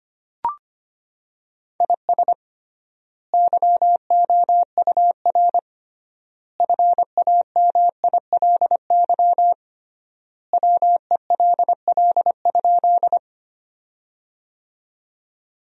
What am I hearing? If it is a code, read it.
Morse code, heard as a keyed tone, about 25 wpm, 700 Hz. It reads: IS YOUR FAMILY WELL?
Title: 3x Word-Spacing - Sentences from Top 400 Words - 25wpm